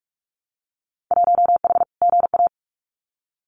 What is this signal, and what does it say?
Morse code, heard as a keyed tone, about 46 wpm, 700 Hz. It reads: 1H GA
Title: Winter Field Day Exchanges 45wpm